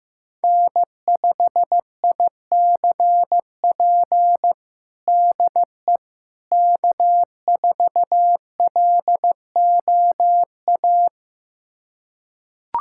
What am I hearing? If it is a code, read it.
Morse code, heard as a keyed tone, about 15 wpm, 700 Hz. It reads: N5ICP DE K4LOA
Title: USA Callsign DE Callsign - 5 Characters - 15wpm